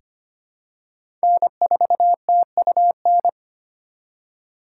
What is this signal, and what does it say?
Morse code, heard as a keyed tone, about 25 wpm, 700 Hz. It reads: N4TUN
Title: USA Call Signs - 5 Characters - 2x Character Spacing - 25wpm